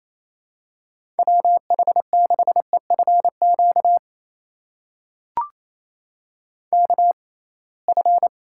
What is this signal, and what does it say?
Morse code, heard as a keyed tone, about 28 wpm, spaced as for 21 wpm, 700 Hz. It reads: WH6EFQ KF
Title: USA Call Signs - 6 Characters - 6x Character Spacing - 28wpm